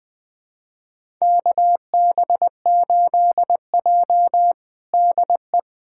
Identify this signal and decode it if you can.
Morse code, heard as a keyed tone, about 20 wpm, 700 Hz. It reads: KB8J DE
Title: USA Callsign DE Callsign - 4 Characters - 20wpm